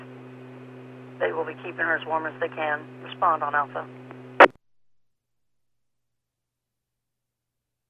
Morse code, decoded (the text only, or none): none